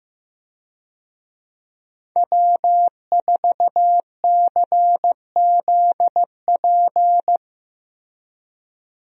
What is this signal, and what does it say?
Morse code, heard as a keyed tone, about 15 wpm, 700 Hz. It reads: W4CZP
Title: USA Call Signs - 5 Characters - Repeated 3x - 15wpm